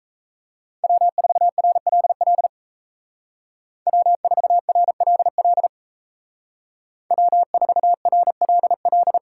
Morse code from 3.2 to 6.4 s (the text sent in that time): W4RLL